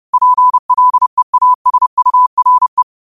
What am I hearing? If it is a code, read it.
Morse code, heard as a keyed tone, about 30 wpm, 1000 Hz. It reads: PLEASURE
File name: qotdfiles / 30wpm